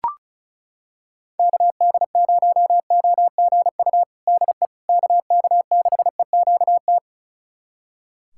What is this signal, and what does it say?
Morse code, heard as a keyed tone, about 36 wpm, 700 Hz. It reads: KD0OGU DE KK6EQT
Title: USA Callsign DE Callsign - 6 Characters - 35wpm